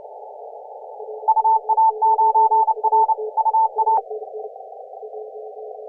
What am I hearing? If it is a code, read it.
Morse code, heard as a keyed tone, about 29 wpm, 890 Hz. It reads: UA9R UA